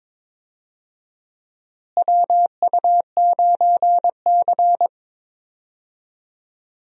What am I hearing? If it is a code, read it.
Morse code, heard as a keyed tone, about 22 wpm, 700 Hz. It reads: WU9C
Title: USA Call Signs - 4 Characters - Repeated 2x - 22wpm